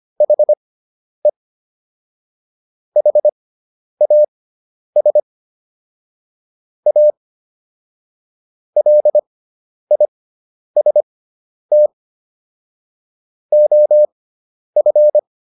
Morse code, heard as a keyed tone, about 25 wpm, spaced as for 10 wpm, 600 Hz. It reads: HE HAS A LIST OF